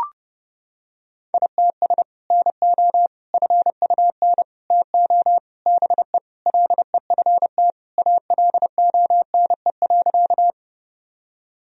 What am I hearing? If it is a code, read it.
Morse code, heard as a keyed tone, about 30 wpm, 700 Hz. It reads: ITS NO FUN TO BE LEFT ALONE.
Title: Sentences from Top 1000 Words - Review Part 4 - 30wpm